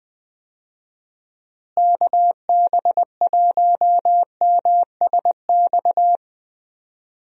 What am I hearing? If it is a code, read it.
Morse code, heard as a keyed tone, about 21 wpm, 700 Hz. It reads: KB1MSX
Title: USA Call Signs - 6 Characters - Repeated 3x - 20wpm